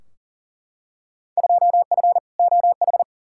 Morse code, heard as a keyed tone, about 40 wpm, 700 Hz. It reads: 2F OH
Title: Field Day Exchanges - 40wpm